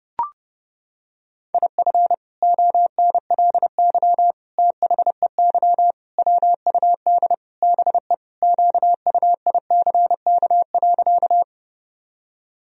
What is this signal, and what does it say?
Morse code, heard as a keyed tone, about 32 wpm, 700 Hz. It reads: IF ONLY THEY WUD BE QUICK.
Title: Sentences from Top 500 Words and Abbreviations 30wpm